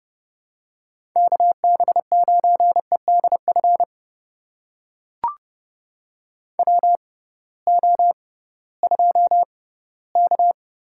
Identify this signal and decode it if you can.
Morse code, heard as a keyed tone, about 30 wpm, spaced as for 18 wpm, 700 Hz. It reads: KB9EDF WO2K
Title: USA Call Signs - 6x Character Spacing - 30wpm